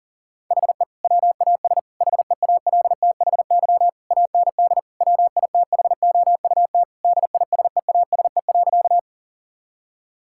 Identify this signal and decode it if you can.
Morse code, heard as a keyed tone, about 37 wpm, 700 Hz. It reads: HE WAS HEALTHY AND WITHOUT DISEASE.
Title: Sentences from Top 1000 Words - Review Part 6 - 40wpm